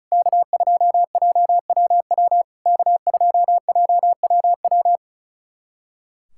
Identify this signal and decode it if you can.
Morse code, heard as a keyed tone, about 36 wpm, 700 Hz. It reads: K2JWW K2JWW
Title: SOTA QSO Practice 35wpm